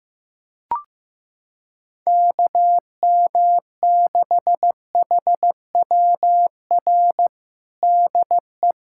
Morse code, heard as a keyed tone, about 15 wpm, 700 Hz. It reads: KM6HWR DE
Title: USA Callsign DE Callsign - 6 Characters - 15wpm